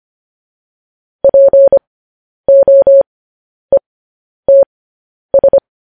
Morse code, heard as a keyed tone, about 25 wpm, spaced as for 10 wpm, 550 Hz. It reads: POETS
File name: podcast / News headlines-1x-25x10